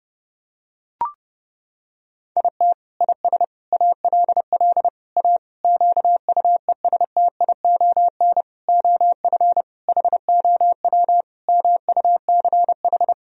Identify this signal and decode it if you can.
Morse code, heard as a keyed tone, about 30 wpm, 700 Hz. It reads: IT IS ALL A QUESTION OF HOW MUCH
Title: Sentences from Top 1000 Words - Review Part 3 - 30wpm